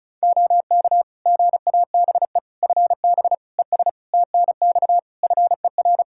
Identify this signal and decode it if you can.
Morse code, heard as a keyed tone, about 36 wpm, 700 Hz. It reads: OK GABE FB ES TNX FER
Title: Practice QSO - Exchange 2 - Info - 35wpm